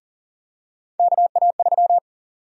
Morse code, heard as a keyed tone, about 38 wpm, 700 Hz. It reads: KA3
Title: USA Call Sign Prefixes - 40wpm